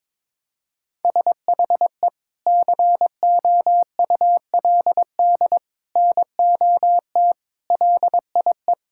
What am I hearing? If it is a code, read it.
Morse code, heard as a keyed tone, about 22 wpm, 700 Hz. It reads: SHE COULD NOT LIE
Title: Sentences from Top 1600 Words 22wpm